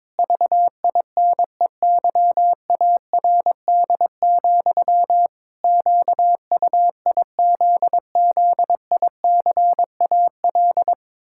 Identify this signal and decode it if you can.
Morse code, heard as a keyed tone, about 22 wpm, 700 Hz. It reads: VINEYARD, QUIZZICAL
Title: Pangram Challenge v2 - 22wpm